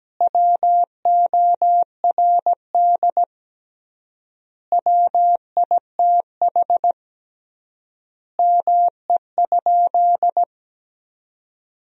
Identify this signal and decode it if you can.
Morse code, heard as a keyed tone, about 17 wpm, 700 Hz. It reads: WORD WITH ME?
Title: 3x Word-Spacing - Sentences from Top 300 Words - 17wpm